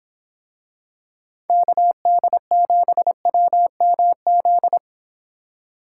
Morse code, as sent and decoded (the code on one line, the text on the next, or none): -.- -.. --... .-- -- --..
KD7WMZ